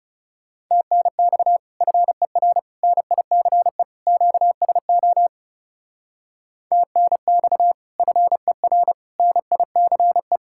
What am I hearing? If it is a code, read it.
Morse code, heard as a keyed tone, about 39 wpm, 700 Hz. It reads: TNX FER NICE QSO TNX FER NICE